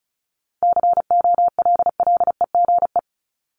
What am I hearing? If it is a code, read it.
Morse code, heard as a keyed tone, about 34 wpm, 700 Hz. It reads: COLLEGE
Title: Top 1001-1250 Words 35wpm